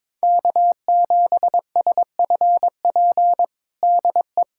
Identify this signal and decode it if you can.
Morse code, heard as a keyed tone, about 22 wpm, 700 Hz. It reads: K7SFP DE
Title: USA Callsign DE Callsign 22wpm